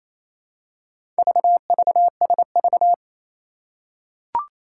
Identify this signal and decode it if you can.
Morse code, heard as a keyed tone, about 27 wpm, 700 Hz. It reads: VVSV
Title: Four Challenging Letter-Number Combinations 28wpm